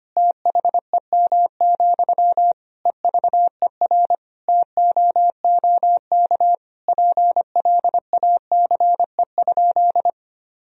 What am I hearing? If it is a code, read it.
Morse code, heard as a keyed tone, about 25 wpm, 700 Hz. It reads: THEM, EVER TOOK PLACE?
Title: Sentences from Top 1000 Words - Review Part 1 - 25wpm